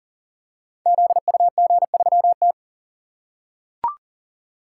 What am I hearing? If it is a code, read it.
Morse code, heard as a keyed tone, about 40 wpm, 700 Hz. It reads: ZUG3T
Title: Code Groups - 5 Letter:Numbers - 40wpm